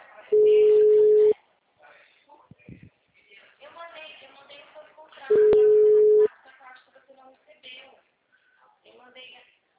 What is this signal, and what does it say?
Ringback tone (Europe, 425 Hz), with speech.